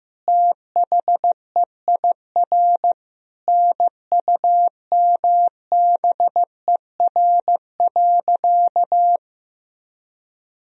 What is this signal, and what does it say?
Morse code, heard as a keyed tone, about 15 wpm, 700 Hz. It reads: THEIR NUMBER.
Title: Sentences from Top 1000 Words - Review Part 1 - 15wpm